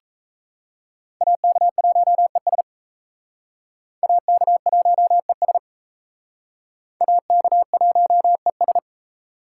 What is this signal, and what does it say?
Morse code, heard as a keyed tone, about 36 wpm, 700 Hz. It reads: AK1ES AK1ES AK1ES